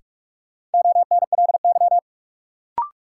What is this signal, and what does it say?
Morse code, heard as a keyed tone, about 46 wpm, 700 Hz. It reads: ONLY
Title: Sentences from Top 200 Words - New Word Drill - 45wpm